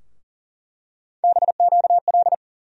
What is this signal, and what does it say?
Morse code, heard as a keyed tone, about 40 wpm, 700 Hz. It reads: BQL
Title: USA Call Sign Postfixes - 3 Letters 40wpm